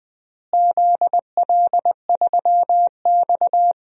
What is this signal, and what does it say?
Morse code, heard as a keyed tone, about 20 wpm, 700 Hz. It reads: ZL3X
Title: DXCC-100 Call Signs - 4 Characters - Repeated 2x - 20wpm